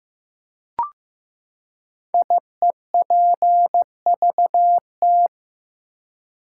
Morse code, heard as a keyed tone, about 15 wpm, 700 Hz. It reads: IEPVT
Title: Code Groups - 5 Letters - 15wpm